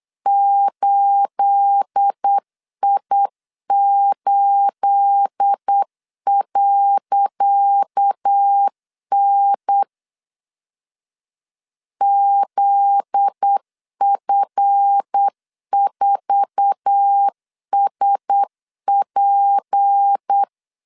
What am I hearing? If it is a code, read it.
Morse code, heard as a keyed tone, about 8 wpm, 800 Hz. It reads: 8I8.N ZF4SP
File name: LEZIONI / 10 SIGNS / 10C DOT 8wpm 3649